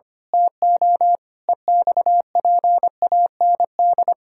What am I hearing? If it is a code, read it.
Morse code, heard as a keyed tone, about 25 wpm, 700 Hz. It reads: TO EXPAND
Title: Pangram Challenge v2 - 25wpm